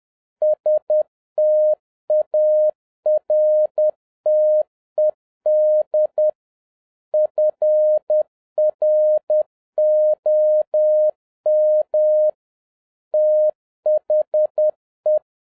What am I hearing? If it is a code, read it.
Morse code, heard as a keyed tone, about 10 wpm, 610 Hz. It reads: STARTED FROM THE
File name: 2011-07-mp3 / n56 BACK FROM A BIG ACCIDENT 10wpm